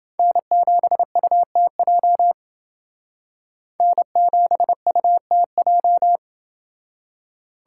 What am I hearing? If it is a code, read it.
Morse code, heard as a keyed tone, about 28 wpm, 700 Hz. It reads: N7UTJ N7UTJ